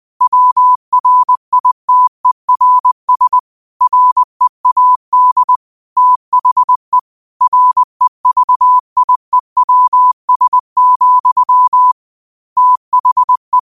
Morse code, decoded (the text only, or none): WRITERS READ THE REVIEWS, THE